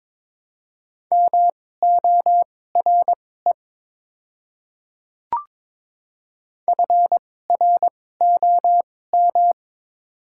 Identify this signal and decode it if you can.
Morse code, heard as a keyed tone, about 22 wpm, spaced as for 16 wpm, 700 Hz. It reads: MORE FROM
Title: 2x Character Spacing - Top 100 Words - 22wpm